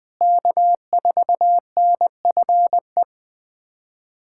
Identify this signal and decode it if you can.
Morse code, heard as a keyed tone, about 21 wpm, 700 Hz. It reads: K4NFE
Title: USA Callsign DE Callsign 20wpm